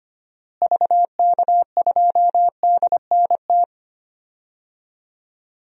Morse code, heard as a keyed tone, about 25 wpm, 700 Hz. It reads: VK2DNT
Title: DXCC-100 Call Signs - 6 Characters - Repeated 3x - 25wpm